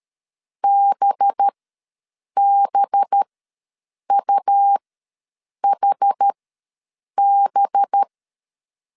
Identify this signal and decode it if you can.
Morse code, heard as a keyed tone, about 13 wpm, spaced as for 7 wpm, 800 Hz. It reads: BBUHB
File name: 5 LB ws 12wpm